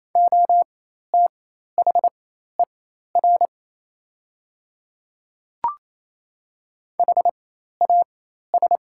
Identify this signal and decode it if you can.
Morse code, heard as a keyed tone, about 28 wpm, spaced as for 13 wpm, 700 Hz. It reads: OTHER HAS
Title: 4x Character Spacing - Top 100 Words - 28wpm